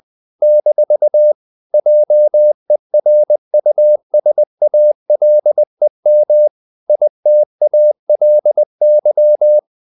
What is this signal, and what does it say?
Morse code, heard as a keyed tone, about 20 wpm, 600 Hz. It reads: - JERUSALEM ITALY
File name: CW / Practice / NationalCapitals20